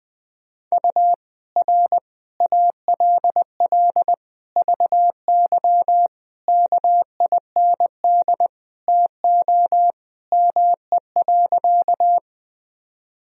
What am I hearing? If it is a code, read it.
Morse code, heard as a keyed tone, about 21 wpm, 700 Hz. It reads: U R ALL VY KIND TO ME.
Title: Sentences from Top 200 Words and Abbreviations 20wpm